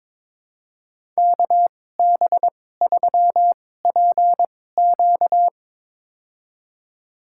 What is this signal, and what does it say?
Morse code, heard as a keyed tone, about 22 wpm, spaced as for 16 wpm, 700 Hz. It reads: KB3PQ
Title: USA Call Signs - 5 Characters - 2x Character Spacing - 22wpm